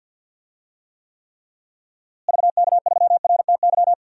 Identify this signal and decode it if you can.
Morse code, heard as a keyed tone, about 51 wpm, 700 Hz. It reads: VK3RTY
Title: DXCC-100 Call Signs - 6 Characters - Repeated 3x - 50wpm